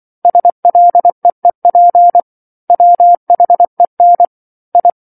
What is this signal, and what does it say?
Morse code, heard as a keyed tone, about 24 wpm, 700 Hz. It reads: SLEEP WHEN I
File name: 07-Chapter-07 24 WPM